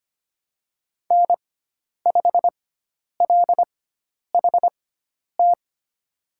Morse code, heard as a keyed tone, about 25 wpm, spaced as for 10 wpm, 700 Hz. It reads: N5LHT
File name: Lesson 16g - Letter c - Call Signs 25wpm